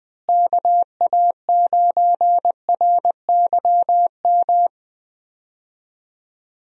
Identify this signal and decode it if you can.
Morse code, heard as a keyed tone, about 20 wpm, 700 Hz. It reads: KA9RYM
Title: USA Call Signs - 6 Characters - Repeated 3x - 20wpm